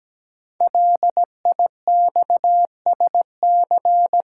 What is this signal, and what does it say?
Morse code, heard as a keyed tone, about 17 wpm, 700 Hz. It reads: LIXSC
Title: Code Groups - 5 Letters - 17wpm